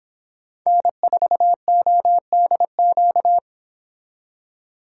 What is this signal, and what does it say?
Morse code, heard as a keyed tone, about 26 wpm, 700 Hz. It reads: N4ODQ